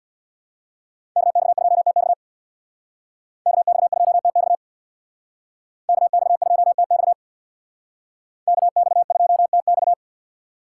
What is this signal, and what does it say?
Morse code, heard as a keyed tone, about 61 wpm, 700 Hz. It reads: KX2TX KX2TX KX2TX KX2TX
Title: Speed Racing - USA 2x2 Call Signs 50wpm